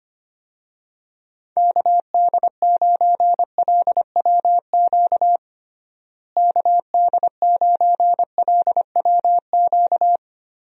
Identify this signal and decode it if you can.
Morse code, heard as a keyed tone, about 25 wpm, 700 Hz. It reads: KD9LWQ KD9LWQ
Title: USA Call Signs - 6 Characters - Repeated 3x - 25wpm